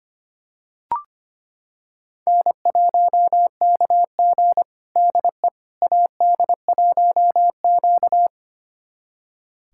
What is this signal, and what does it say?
Morse code, heard as a keyed tone, about 25 wpm, 700 Hz. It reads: N1KG DE AD1Q
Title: USA Callsign DE Callsign - 4 Characters - 25wpm